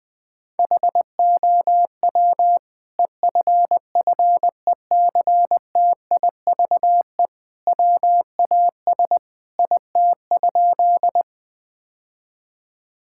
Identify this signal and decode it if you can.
Morse code, heard as a keyed tone, about 20 wpm, 700 Hz. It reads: HOW EFFECTIVE WAS IT?
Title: Sentences from Top 1000 Words - Review Part 6 - 20wpm